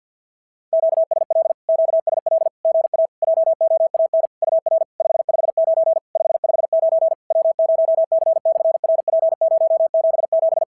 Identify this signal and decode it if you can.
Morse code, heard as a keyed tone, about 46 wpm, 660 Hz. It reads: QSL QSL GA JOAN UR 559 559 W0C/RP077
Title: SOTA QSO Practice 50wpm